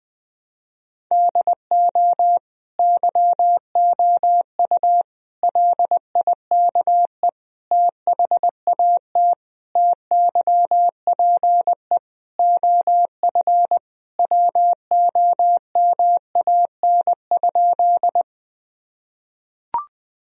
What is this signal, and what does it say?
Morse code, heard as a keyed tone, about 20 wpm, 700 Hz. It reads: DO YOU LIKE THAT TYPE OF WOMAN?
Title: Sentences from Top 1000 Words - Review Part 3 - 20wpm